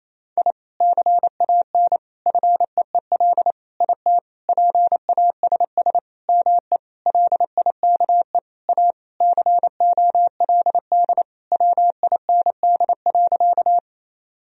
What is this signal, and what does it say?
Morse code, heard as a keyed tone, about 28 wpm, 700 Hz. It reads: I CAN FEEL IT PASS ME LIKE A COLD WIND.
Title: Sentences from Top 1000 Words - Review Part 2 - 28wpm